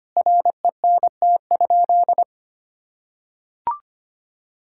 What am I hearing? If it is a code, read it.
Morse code, heard as a keyed tone, about 25 wpm, 700 Hz. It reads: RENT?
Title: Sentences from Top 1000 Words - Review Part 5 - 25wpm